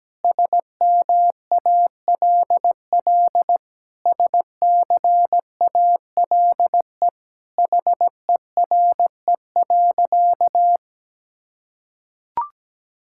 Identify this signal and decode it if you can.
Morse code, heard as a keyed tone, about 17 wpm, 700 Hz. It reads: SMALL SCALE HERE.
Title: Sentences from Top 1000 Words - Review Part 5 - 17wpm